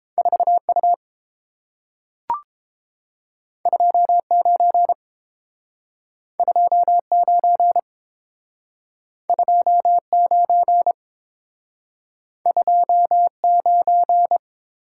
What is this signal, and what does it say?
Morse code, heard as a keyed tone, about 30 wpm, 700 Hz. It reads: 4U 29 29 29 29